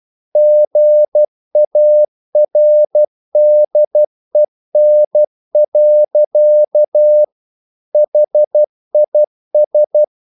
Morse code, heard as a keyed tone, about 12 wpm, 600 Hz. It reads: GARDEN. HIS
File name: mp3 files / the gardener - 12wpm